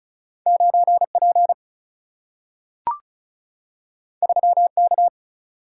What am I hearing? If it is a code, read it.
Morse code, heard as a keyed tone, about 36 wpm, 700 Hz. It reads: 9P 3K
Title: Single-Letter Single-Number 35wpm